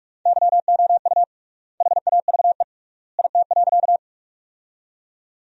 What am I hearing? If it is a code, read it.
Morse code, heard as a keyed tone, about 46 wpm, 700 Hz. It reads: YOU HAVE IT.
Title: 3x Word-Spacing - Sentences from Top 100 Words - 45wpm